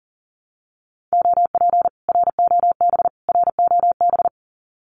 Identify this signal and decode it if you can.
Morse code, heard as a keyed tone, about 40 wpm, 700 Hz. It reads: OP ROB ROB
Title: Common QSO Phrases Encore 40wpm